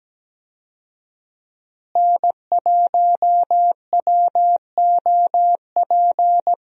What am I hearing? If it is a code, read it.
Morse code, heard as a keyed tone, about 17 wpm, 700 Hz. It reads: N1WOP
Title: USA Call Signs - 5 Characters - Repeated 3x - 17wpm